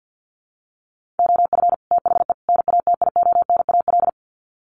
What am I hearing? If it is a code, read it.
Morse code, heard as a keyed tone, about 48 wpm, 700 Hz. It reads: OF THE NATIONAL
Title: Sets of 3 Words 50wpm